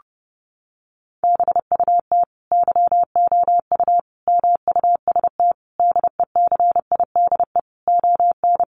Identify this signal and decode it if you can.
Morse code, heard as a keyed tone, about 30 wpm, 700 Hz. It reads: BUT YOU MUST DECIDE ON
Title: Sentences from Top 600 Words 30wpm